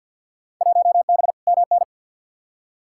Morse code, heard as a keyed tone, about 52 wpm, 700 Hz. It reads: 1B MN
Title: Field Day Exchanges - 50wpm